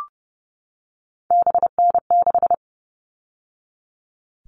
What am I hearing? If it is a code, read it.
Morse code, heard as a keyed tone, about 30 wpm, 700 Hz. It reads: BN6
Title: All Cut Numbers - 3-Digit 30wpm